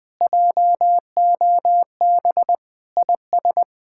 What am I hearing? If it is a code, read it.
Morse code, heard as a keyed tone, about 20 wpm, 700 Hz. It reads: JOB IS
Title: Speed-Racing Common QSO Phrases - 20wpm